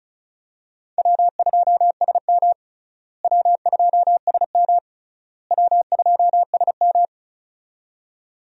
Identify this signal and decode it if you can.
Morse code, heard as a keyed tone, about 36 wpm, 700 Hz. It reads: W2SM W2SM W2SM
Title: USA Call Signs - 4 Characters - Repeated 3x - 35wpm